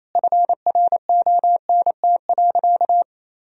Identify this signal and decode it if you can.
Morse code, heard as a keyed tone, about 28 wpm, 700 Hz. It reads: FRONT.
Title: Sentences from Top 1000 Words - Review Part 2 - 28wpm